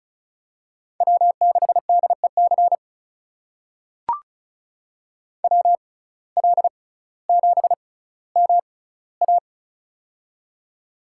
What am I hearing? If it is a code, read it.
Morse code, heard as a keyed tone, about 36 wpm, spaced as for 18 wpm, 700 Hz. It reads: W6DEC WL7MA
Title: USA Call Signs - 5 Characters - 6x Character Spacing - 35wpm